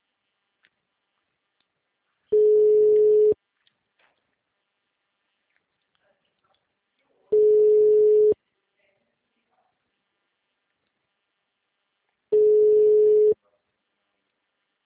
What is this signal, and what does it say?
Ringback tone (Europe, 425 Hz), with speech.